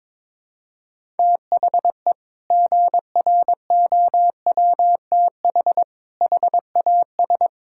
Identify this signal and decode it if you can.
Morse code, heard as a keyed tone, about 21 wpm, 700 Hz. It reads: THE GROWTH HAS
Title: Sentences from Top 1000 Words - Review Part 5 - 22wpm